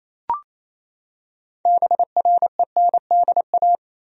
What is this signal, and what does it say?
Morse code, heard as a keyed tone, about 28 wpm, 700 Hz. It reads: BRENDA
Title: USA Common Names - 28wpm